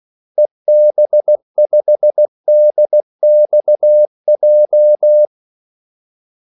Morse code, heard as a keyed tone, about 16 wpm, 600 Hz. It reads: EB5DXJ
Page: Kaldesignaler 16 wpm